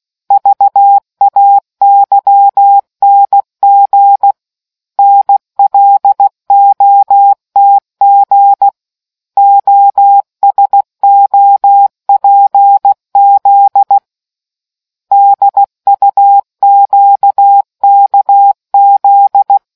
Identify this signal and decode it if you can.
Morse code, heard as a keyed tone, about 16 wpm, 800 Hz. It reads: VAYNG NLOTG OSOPZ DUQKZ